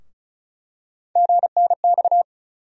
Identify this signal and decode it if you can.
Morse code, heard as a keyed tone, about 36 wpm, 700 Hz. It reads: GNX
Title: USA Call Sign Postfixes - 3 Letters 35wpm